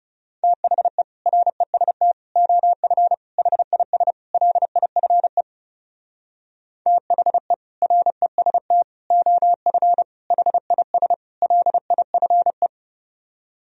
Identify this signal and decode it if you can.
Morse code, heard as a keyed tone, about 36 wpm, 700 Hz. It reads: THE REST OF HIS LIFE THE REST OF HIS LIFE
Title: Speed-Racing - Sets of 5 Words 30wpm